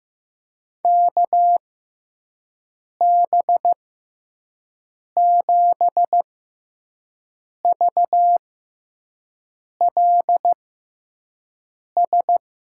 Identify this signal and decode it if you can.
Morse code, heard as a keyed tone, about 15 wpm, spaced as for 5 wpm, 700 Hz. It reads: KB7VLS
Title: USA Call Signs - 6 Characters - 6x Character Spacing - 15wpm